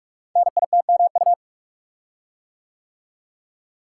Morse code, heard as a keyed tone, about 44 wpm, 700 Hz. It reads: NITMU
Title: Code Groups - 5 Letter:Numbers - 45wpm